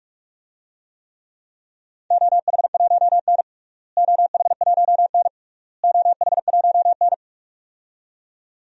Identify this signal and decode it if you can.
Morse code, heard as a keyed tone, about 41 wpm, 700 Hz. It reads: OH1N OH1N OH1N
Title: DXCC-100 Call Signs - 4 Characters - Repeated 3x - 45wpm